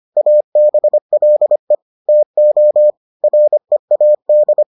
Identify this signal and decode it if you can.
Morse code, heard as a keyed tone, about 25 wpm, 600 Hz. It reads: ABLE TO READ